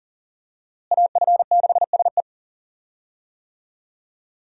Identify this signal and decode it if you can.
Morse code, heard as a keyed tone, about 40 wpm, 700 Hz. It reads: AF6SE
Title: USA Call Signs - 5 Characters - Repeated 3x - 40wpm